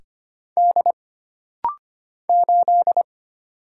Morse code, heard as a keyed tone, about 25 wpm, 700 Hz. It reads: D 8
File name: Single Character (Letter:Number:Punctuation:Prosign) - Rapid-Fire - 25wpm